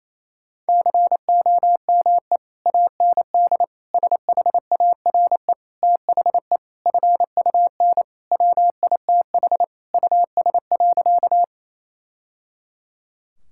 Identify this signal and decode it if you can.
Morse code, heard as a keyed tone, about 27 wpm, 700 Hz. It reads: COME AND SHARE THE FUN WITH US.
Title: Sentences from Top 1000 Words - Review Part 6 - 28wpm